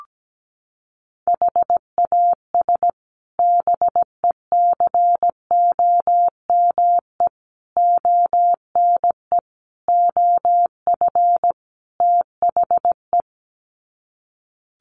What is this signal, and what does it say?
Morse code, heard as a keyed tone, about 17 wpm, 700 Hz. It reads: HAS BECOME ONE OF THE
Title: Sets of 5 Words Encore 17wpm